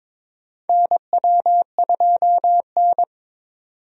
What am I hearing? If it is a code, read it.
Morse code, heard as a keyed tone, about 21 wpm, 700 Hz. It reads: NW2N
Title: USA Call Signs - 4 Characters - 22wpm